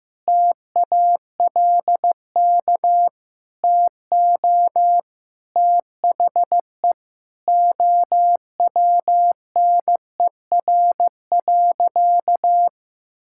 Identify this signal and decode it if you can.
Morse code, heard as a keyed tone, about 15 wpm, 700 Hz. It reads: TALK TO THE OWNER.
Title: Sentences from Top 1600 Words 15wpm